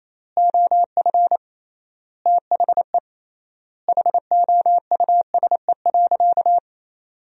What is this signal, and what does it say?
Morse code, heard as a keyed tone, about 28 wpm, 700 Hz. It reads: OF THE HOUSE.
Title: 3x Word-Spacing - Sentences from Top 400 Words - 28wpm